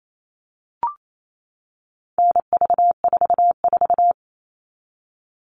Morse code, heard as a keyed tone, about 28 wpm, 700 Hz. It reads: NV44
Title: All Cut Numbers - 4-Digit 28wpm